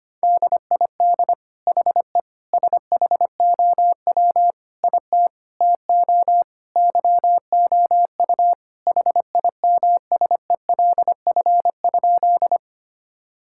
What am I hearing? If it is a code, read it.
Morse code, heard as a keyed tone, about 25 wpm, 700 Hz. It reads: DID HE SHOW IT TO YOU HIMSELF?
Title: Sentences from Top 1600 Words 25wpm